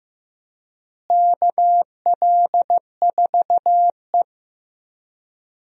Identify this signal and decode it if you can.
Morse code, heard as a keyed tone, about 15 wpm, 700 Hz. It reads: KL4E
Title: USA Call Signs - 4 Characters - Repeated 2x - 15wpm